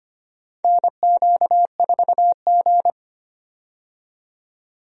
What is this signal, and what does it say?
Morse code, heard as a keyed tone, about 25 wpm, 700 Hz. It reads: NQ4G
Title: USA Call Signs - 4 Characters - 25wpm